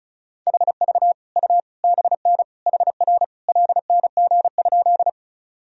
Morse code, heard as a keyed tone, about 36 wpm, 690 Hz. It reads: HV U BN HR LNG?
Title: Sentences from Top 200 Words and Abbreviations 35wpm